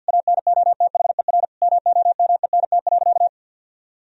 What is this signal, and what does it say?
Morse code, heard as a keyed tone, about 52 wpm, 700 Hz. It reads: ANOTHER MOMENT.
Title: Sentences from Top 1000 Words - Review Part 1 - 50wpm